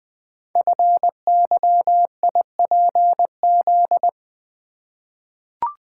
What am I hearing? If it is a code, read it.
Morse code, heard as a keyed tone, about 20 wpm, 700 Hz. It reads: FYIPZ
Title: Code Groups - 5 Letter:Numbers - 20wpm